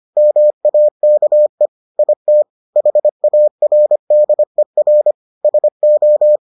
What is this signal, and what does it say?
Morse code, heard as a keyed tone, about 25 wpm, 600 Hz. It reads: MAKE IT HARDER SO